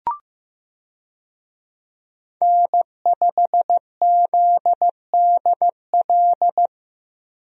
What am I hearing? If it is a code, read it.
Morse code, heard as a keyed tone, about 16 wpm, 700 Hz. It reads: N5ZDL
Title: USA Call Signs - 5 Characters - Repeated 3x - 15wpm